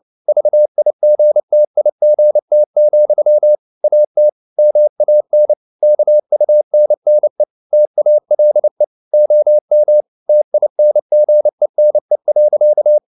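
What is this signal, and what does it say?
Morse code, heard as a keyed tone, about 29 wpm, 600 Hz. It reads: VIGTIGT, AT MAN KUNNE TALE OM TINGENE.
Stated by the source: Fast (QRQ) Morse at 29 wpm, from CW med Gnister.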